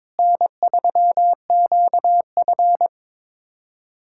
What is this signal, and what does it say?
Morse code, heard as a keyed tone, about 22 wpm, 700 Hz. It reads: N3QF
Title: USA Callsign DE Callsign - 4 Characters - 22wpm